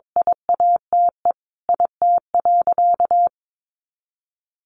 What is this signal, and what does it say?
Morse code, heard as a keyed tone, about 22 wpm, 700 Hz. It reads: IATE IT.
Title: Sentences from Top 1300 Words 22wpm